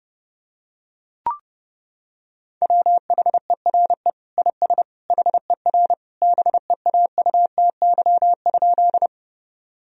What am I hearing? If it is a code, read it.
Morse code, heard as a keyed tone, about 30 wpm, 700 Hz. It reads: WHERE IS HER BEAUTY?
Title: Sentences from Top 1000 Words - Review Part 2 - 30wpm